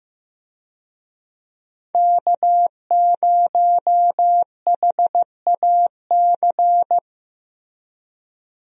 Morse code, heard as a keyed tone, about 15 wpm, 700 Hz. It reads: K0HAC
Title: USA Call Signs - 5 Characters - Repeated 3x - 15wpm